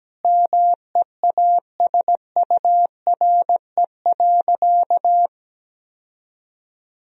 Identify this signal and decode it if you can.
Morse code, heard as a keyed tone, about 17 wpm, 700 Hz. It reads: MEASURE.
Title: Sentences from Top 1000 Words - Review Part 3 - 17wpm